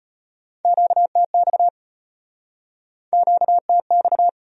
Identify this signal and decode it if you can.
Morse code, heard as a keyed tone, about 34 wpm, 700 Hz. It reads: QTX QTX